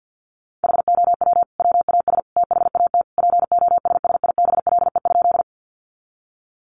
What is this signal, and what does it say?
Morse code, heard as a keyed tone, about 50 wpm, 700 Hz. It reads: HOW WAS THAT POSSIBLE?
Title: Sentences from Top 500 Words 50wpm